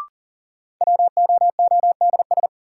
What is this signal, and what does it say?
Morse code, heard as a keyed tone, about 40 wpm, 700 Hz. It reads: WOODS
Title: Sentences from Top 1000 Words - Review Part 2 - New Word Drill - 40wpm